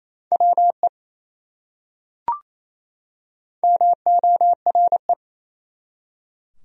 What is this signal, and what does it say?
Morse code, heard as a keyed tone, about 28 wpm, 700 Hz. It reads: WE MORE
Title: Top 100 Words 28wpm